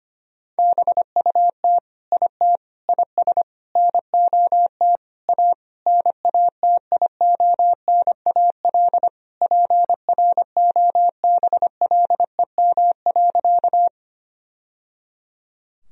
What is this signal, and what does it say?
Morse code, heard as a keyed tone, about 25 wpm, 700 Hz. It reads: BUT IT IS NOT A NATIONAL PROBLEM.
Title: Sentences from Top 1000 Words - Review Part 4 - 25wpm